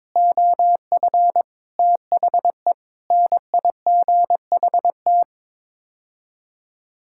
Morse code, heard as a keyed tone, about 22 wpm, 700 Hz. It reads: OF THE NIGHT
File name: practice / 22wpm / Sets of 8 Words - 22wpm